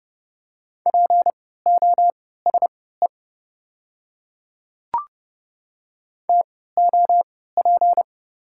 Morse code, heard as a keyed tone, about 32 wpm, spaced as for 17 wpm, 690 Hz. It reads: POSE TOP
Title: 3x Character Spacing - Top 500 Words - 30wpm